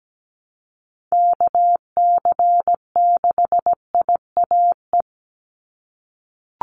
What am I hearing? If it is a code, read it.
Morse code, heard as a keyed tone, about 17 wpm, 700 Hz. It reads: KC6IAE
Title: USA Call Signs 17wpm